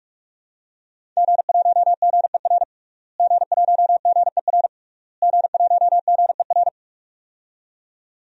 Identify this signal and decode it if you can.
Morse code, heard as a keyed tone, about 46 wpm, 700 Hz. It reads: G1GER G1GER G1GER
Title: DXCC-100 Call Signs - 5 Characters - Repeated 3x - 45wpm